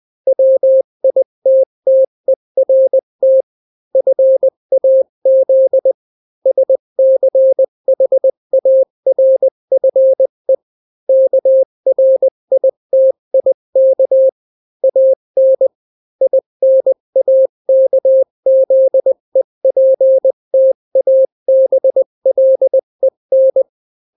Morse code, heard as a keyed tone, about 20 wpm, 530 Hz. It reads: WITTERT FAZ SCHARFE KRITIK AN INAKZEPTABLEN
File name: daily / 18wpm